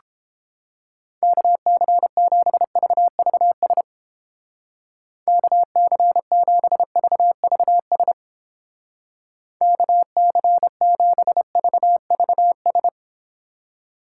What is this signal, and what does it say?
Morse code, heard as a keyed tone, about 29 wpm, 700 Hz. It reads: KC7VVS KC7VVS KC7VVS